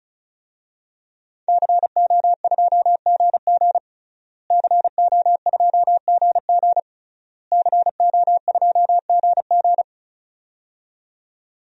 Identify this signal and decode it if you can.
Morse code, heard as a keyed tone, about 36 wpm, 700 Hz. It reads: CO2GG CO2GG CO2GG
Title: DXCC-100 Call Signs - 5 Characters - Repeated 3x - 35wpm